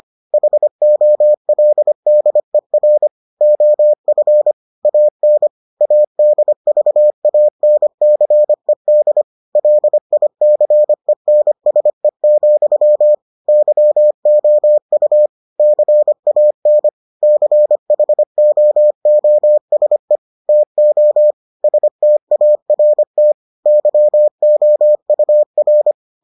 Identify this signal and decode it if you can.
Morse code, heard as a keyed tone, about 25 wpm, 600 Hz. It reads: HOLDER OF AN ADVANCED LICENSE, YOU CAN CHOOSE TO START YOUR